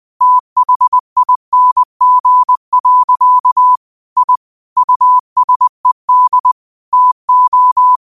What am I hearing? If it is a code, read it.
Morse code, heard as a keyed tone, about 20 wpm, 1000 Hz. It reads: THING. I USED TO